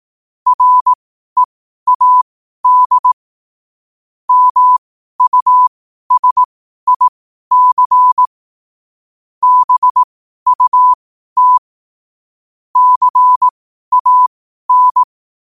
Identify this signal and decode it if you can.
Morse code, heard as a keyed tone, about 18 wpm, spaced as for 12 wpm, 1000 Hz. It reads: READ MUSIC BUT CAN